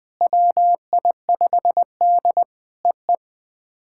Morse code, heard as a keyed tone, about 18 wpm, 700 Hz. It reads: WI5D EE
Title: POTA QSO Practice - 20wpm